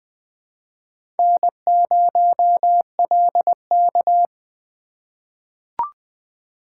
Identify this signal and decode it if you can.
Morse code, heard as a keyed tone, about 20 wpm, 700 Hz. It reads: N0LK
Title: USA Call Signs - 4 Characters - 2x Character Spacing - 20wpm